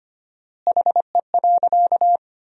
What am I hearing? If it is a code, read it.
Morse code, heard as a keyed tone, about 25 wpm, 700 Hz. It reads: HE.
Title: 4x Word-Spacing - Sentences from Top 400 Words - 25wpm